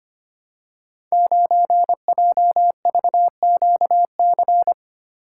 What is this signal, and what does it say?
Morse code, heard as a keyed tone, about 25 wpm, 700 Hz. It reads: 9JVQC
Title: Code Groups - 5 Letter:Numbers - 25wpm